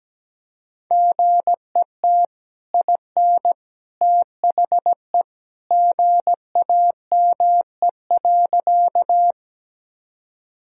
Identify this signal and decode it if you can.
Morse code, heard as a keyed tone, about 17 wpm, 700 Hz. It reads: GET IN THE GAME.
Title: Sentences from Top 1000 Words - Review Part 3 - 17wpm